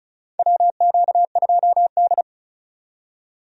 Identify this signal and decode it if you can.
Morse code, heard as a keyed tone, about 36 wpm, 700 Hz. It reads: WQ2D
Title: USA Call Signs - 4 Characters - 35wpm